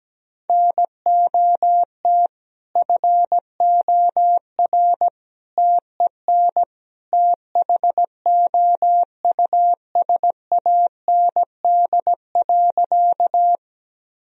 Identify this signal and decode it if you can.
Morse code, heard as a keyed tone, about 17 wpm, 700 Hz. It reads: NOT FOR TEN THOUSAND.
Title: Sentences from Top 1000 Words - Review Part 3 - 17wpm